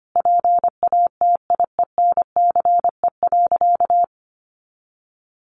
Morse code, heard as a keyed tone, about 25 wpm, 700 Hz. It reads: PATIENCE.
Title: Sentences from Top 1300 Words 25wpm